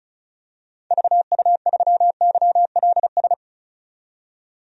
Morse code, heard as a keyed tone, about 36 wpm, 700 Hz. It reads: VU3YLS
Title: DXCC-100 Call Signs - 6 Characters - 35wpm